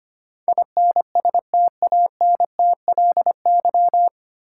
Morse code, heard as a keyed tone, about 25 wpm, 700 Hz. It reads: INSTANTLY
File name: Nine Letter Words - 25wpm